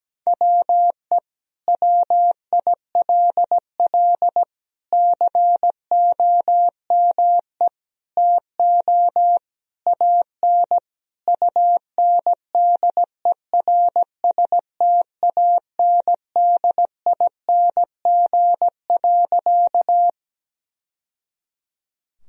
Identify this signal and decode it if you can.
Morse code, heard as a keyed tone, about 17 wpm, 700 Hz. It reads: WE WILL COME TO AN UNDERSTANDING.
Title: Sentences from Top 1000 Words - Review Part 3 - 17wpm